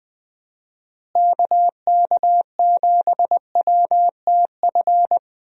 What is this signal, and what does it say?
Morse code, heard as a keyed tone, about 20 wpm, 700 Hz. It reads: KK7WTF
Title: USA Call Signs - 6 Characters - Repeated 2x - 20wpm